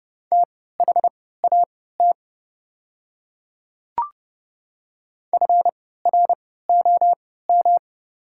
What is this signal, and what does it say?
Morse code, heard as a keyed tone, about 30 wpm, spaced as for 17 wpm, 700 Hz. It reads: THAT FROM
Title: 3x Character Spacing - Top 200 Words - 30wpm